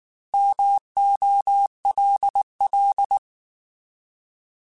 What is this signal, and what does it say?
Morse code, heard as a keyed tone, about 19 wpm, 800 Hz. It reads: MOLL